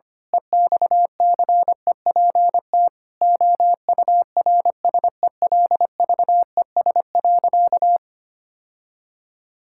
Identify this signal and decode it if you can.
Morse code, heard as a keyed tone, about 25 wpm, 700 Hz. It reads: EXCEPT OURSELVES.
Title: Sentences from Top 1600 Words 25wpm